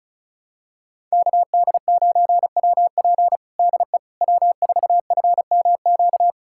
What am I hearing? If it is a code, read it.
Morse code, heard as a keyed tone, about 34 wpm, 690 Hz. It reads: KD9WP DE W4FMQ
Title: USA Callsign DE Callsign - 5 Characters - 35wpm